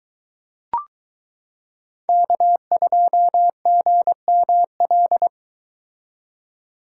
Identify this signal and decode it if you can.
Morse code, heard as a keyed tone, about 22 wpm, 700 Hz. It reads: K2GML